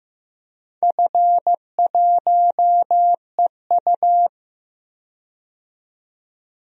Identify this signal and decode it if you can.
Morse code, heard as a keyed tone, about 15 wpm, 700 Hz. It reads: F1EU
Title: DXCC-100 Call Signs - 4 Characters - 15wpm